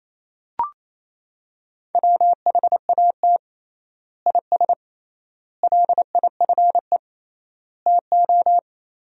Morse code, heard as a keyed tone, about 27 wpm, 700 Hz. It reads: WHAT IS LIFE TO
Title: 3x Word-Spacing - Sentences from Top 300 Words - 28wpm